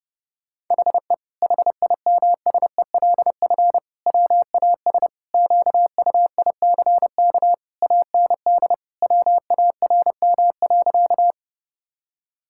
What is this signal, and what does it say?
Morse code, heard as a keyed tone, about 30 wpm, 700 Hz. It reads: HE HIMSELF WAS QUICK AND WARM.
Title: Sentences from Top 1600 Words 30wpm